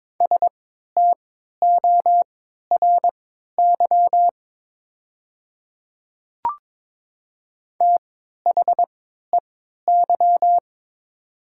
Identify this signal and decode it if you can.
Morse code, heard as a keyed tone, about 22 wpm, spaced as for 12 wpm, 700 Hz. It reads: STORY THEY
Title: 3x Character Spacing - Top 300 Words - 22wpm